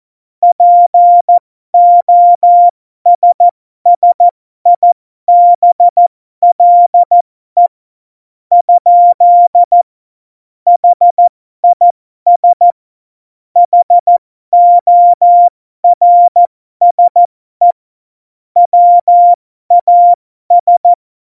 Morse code, heard as a keyed tone, about 14 wpm, spaced as for 12 wpm, 700 Hz. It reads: POSSIBLE? HIS HORSE WAS
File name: Morsekurs / Witze